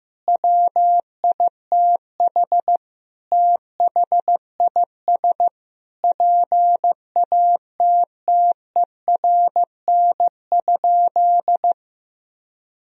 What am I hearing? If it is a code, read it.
Morse code, heard as a keyed tone, about 16 wpm, 700 Hz. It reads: WITH THIS PATTERN?
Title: Sentences from Top 1000 Words - Review Part 2 - 15wpm